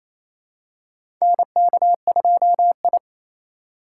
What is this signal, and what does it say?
Morse code, heard as a keyed tone, about 28 wpm, 700 Hz. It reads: NK2I
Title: USA Call Signs - 4 Characters - 2x Character Spacing - 28wpm